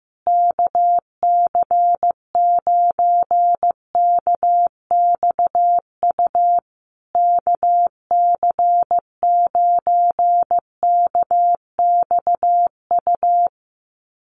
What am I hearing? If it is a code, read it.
Morse code, heard as a keyed tone, about 15 wpm, 700 Hz. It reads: KC9KXU KC9KXU
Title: USA Call Signs Repeated 3x 15wpm